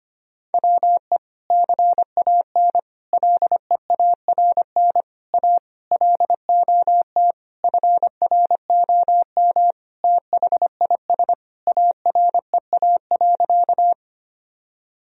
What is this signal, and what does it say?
Morse code, heard as a keyed tone, about 25 wpm, 700 Hz. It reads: WE CAN LEARN A LOT FROM THIS AREA.
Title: Sentences from Top 1000 Words - Review Part 2 - 25wpm